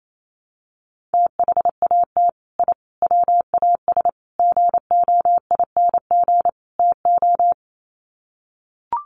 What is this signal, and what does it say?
Morse code, heard as a keyed tone, about 28 wpm, 700 Hz. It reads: THAT I WAS GOING TO
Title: Sets of 5 Words Encore 28wpm